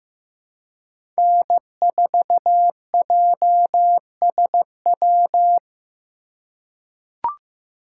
Morse code, heard as a keyed tone, about 15 wpm, 700 Hz. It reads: N4JSW
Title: USA Call Signs - 5 Characters - 4x Character Spacing - 15wpm